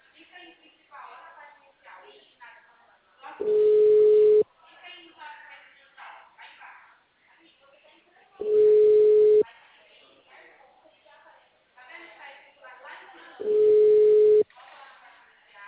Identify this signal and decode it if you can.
Ringback tone (Europe, 425 Hz), with speech.